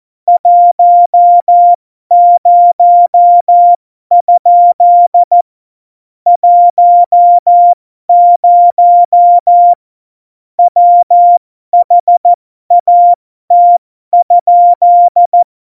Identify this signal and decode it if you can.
Morse code, heard as a keyed tone, about 14 wpm, spaced as for 12 wpm, 700 Hz. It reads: 10? 10 WHAT?
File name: Morsekurs / Witze